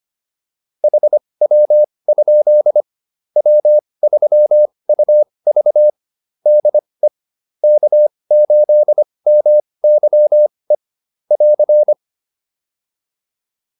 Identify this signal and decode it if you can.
Morse code, heard as a keyed tone, about 25 wpm, spaced as for 20 wpm, 600 Hz. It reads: HW? W3UV DE K8MYE <AR>